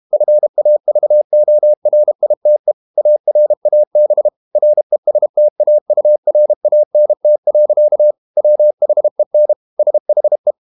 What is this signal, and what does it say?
Morse code, heard as a keyed tone, about 32 wpm, 600 Hz. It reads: FAVORITE ARAB RESTAURANT. WHEN SHE
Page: Never 31wpm | CW med Gnister